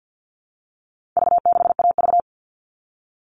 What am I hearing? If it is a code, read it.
Morse code, heard as a keyed tone, about 50 wpm, 700 Hz. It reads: V6AV
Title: All Cut Numbers - 4-Digit 50wpm